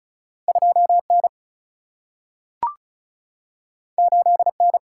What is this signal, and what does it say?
Morse code, heard as a keyed tone, about 36 wpm, 700 Hz. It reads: 2 N 8 N
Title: Single-Letter Single-Number 35wpm